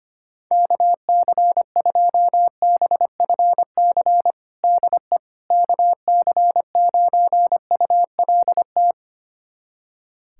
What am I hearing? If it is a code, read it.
Morse code, heard as a keyed tone, about 25 wpm, 700 Hz. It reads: KC2BFC DE KC9ULT
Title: USA Callsign DE Callsign - 6 Characters - 25wpm